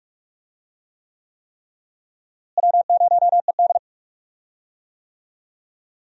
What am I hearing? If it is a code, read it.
Morse code, heard as a keyed tone, about 48 wpm, 700 Hz. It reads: W0ED
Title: USA Call Signs - 4 Characters - Repeated 3x - 45wpm